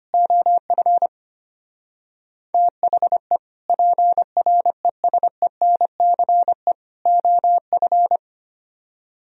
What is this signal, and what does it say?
Morse code, heard as a keyed tone, about 32 wpm, 700 Hz. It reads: OF THE PRESENCE OF